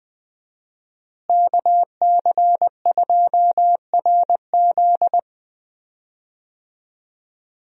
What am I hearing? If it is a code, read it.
Morse code, heard as a keyed tone, about 20 wpm, 700 Hz. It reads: KC2RZ
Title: USA Call Signs - 5 Characters - Repeated 2x - 20wpm